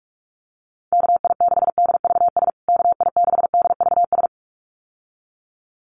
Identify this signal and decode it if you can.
Morse code, heard as a keyed tone, about 44 wpm, 700 Hz. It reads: KI6DVS KI6DVS
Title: USA Call Signs Repeated 2x 45wpm